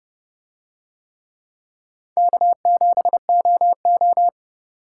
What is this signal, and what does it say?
Morse code, heard as a keyed tone, about 30 wpm, 700 Hz. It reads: K7OO